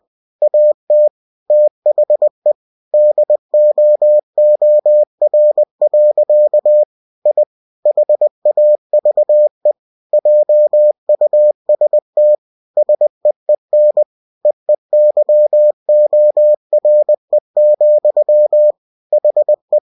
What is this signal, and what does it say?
Morse code, heard as a keyed tone, about 20 wpm, 600 Hz. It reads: AT THE DOOR. I HAVE JUST SEEN EEYORE, HE
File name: cw-books / winnie-the-pooh / winnie-20wpm-page-0048